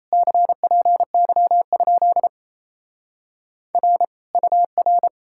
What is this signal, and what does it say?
Morse code, heard as a keyed tone, about 37 wpm, 700 Hz. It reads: CPY? R UR